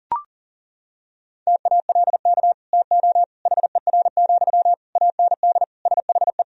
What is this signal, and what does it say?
Morse code, heard as a keyed tone, about 37 wpm, 700 Hz. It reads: TALK TO HER, AND SHE
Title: Sentences from Top 1000 Words - Review Part 3 - 40wpm